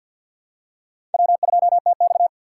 Morse code, heard as a keyed tone, about 50 wpm, 700 Hz. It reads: W2TX
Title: USA Call Signs - 4 Characters - 2x Character Spacing - 50wpm